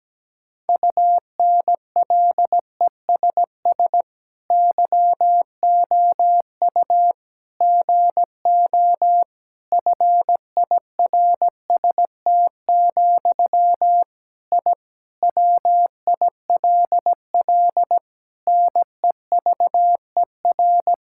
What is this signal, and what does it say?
Morse code, heard as a keyed tone, about 17 wpm, 700 Hz. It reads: UNLESS YOU GO FIRST, I WILL NEVER
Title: Sentences from Top 1000 Words - Review Part 6 - 17wpm